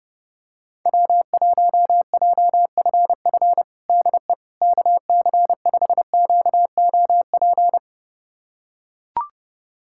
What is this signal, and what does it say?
Morse code, heard as a keyed tone, about 32 wpm, 700 Hz. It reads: W1JFF DE KC5QOP
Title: USA Callsign DE Callsign 30wpm